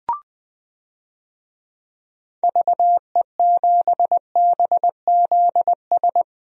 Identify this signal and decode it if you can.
Morse code, heard as a keyed tone, about 20 wpm, 700 Hz. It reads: VE7BZS
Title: DXCC-100 Call Signs - 6 Characters - Repeated 2x - 20wpm